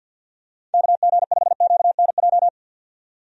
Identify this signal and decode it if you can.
Morse code, heard as a keyed tone, about 46 wpm, 700 Hz. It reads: KG5QNJ
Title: USA Call Signs - 6 Characters - 50wpm